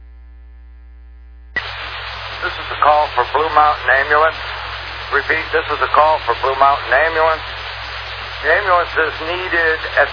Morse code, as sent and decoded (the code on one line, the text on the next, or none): none